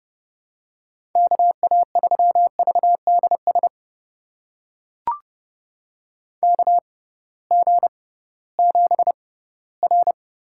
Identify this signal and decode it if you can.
Morse code, heard as a keyed tone, about 29 wpm, spaced as for 17 wpm, 700 Hz. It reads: KA3VDS KG7R
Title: USA Call Signs - 6 Characters - 6x Character Spacing - 30wpm